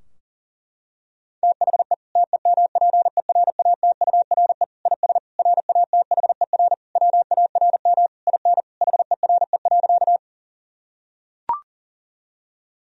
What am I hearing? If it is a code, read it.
Morse code, heard as a keyed tone, about 37 wpm, 700 Hz. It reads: THE TEMPERATURE IS RATHER WARM IN HERE.
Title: Sentences from Top 1000 Words - Review Part 4 - 40wpm